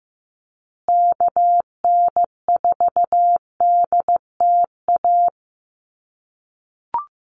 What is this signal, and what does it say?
Morse code, heard as a keyed tone, about 15 wpm, 700 Hz. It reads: KN4DTA
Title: USA Call Signs Repeated 2x 15wpm